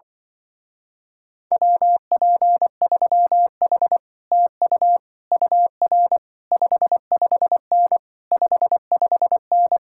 Morse code, heard as a keyed tone, about 24 wpm, 700 Hz. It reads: WP3H TU UR 55N 55N
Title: Speed-Racing Common QSO Phrases Encore - 20wpm